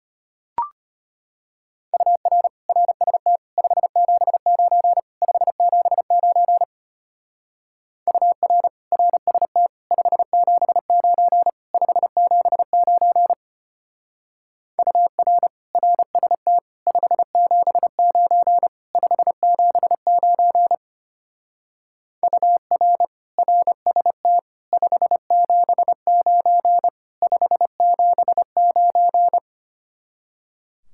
Morse code, heard as a keyed tone, about 32 wpm, 700 Hz. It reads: UR RST 579 579 UR RST 579 579 UR RST 579 579 UR RST 579 579